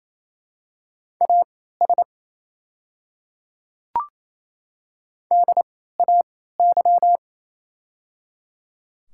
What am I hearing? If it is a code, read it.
Morse code, heard as a keyed tone, about 28 wpm, spaced as for 16 wpm, 700 Hz. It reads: AS DAY
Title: 3x Character Spacing - Top 200 Words - 28wpm